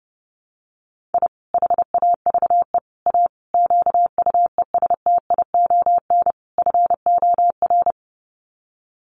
Morse code, heard as a keyed tone, about 30 wpm, 700 Hz. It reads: I HAVE A QUESTION FOR
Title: Sets of 5 Words Encore2 30wpm